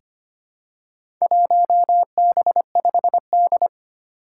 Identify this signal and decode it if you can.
Morse code, heard as a keyed tone, about 25 wpm, 700 Hz. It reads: 1B5D
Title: Four Challenging Letter-Number Combinations 25wpm